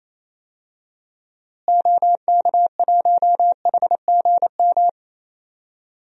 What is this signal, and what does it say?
Morse code, heard as a keyed tone, about 27 wpm, 700 Hz. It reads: OK1HGM
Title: DXCC-100 Call Signs - 6 Characters - Repeated 2x - 28wpm